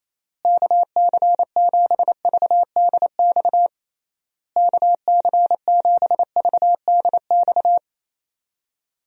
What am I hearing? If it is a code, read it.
Morse code, heard as a keyed tone, about 28 wpm, 700 Hz. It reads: KC7VDX KC7VDX
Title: USA Call Signs - 6 Characters - Repeated 3x - 28wpm